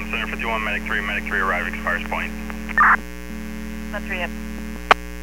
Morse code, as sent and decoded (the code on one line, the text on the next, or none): none